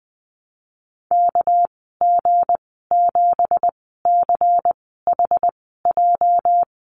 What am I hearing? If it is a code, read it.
Morse code, heard as a keyed tone, about 20 wpm, spaced as for 15 wpm, 700 Hz. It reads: KG7CHJ
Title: 219 Lesson 35g - Number 0 - Call Signs 20wpm